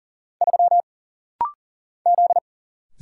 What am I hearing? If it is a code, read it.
Morse code, heard as a keyed tone, about 40 wpm, 700 Hz. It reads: 3 Z
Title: Single Letter-Number Alt - Rapid-Fire 40wpm